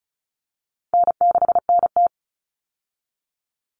Morse code, heard as a keyed tone, about 34 wpm, 700 Hz. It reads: N6NT
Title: Common Cut Numbers - 4-Digit 35wpm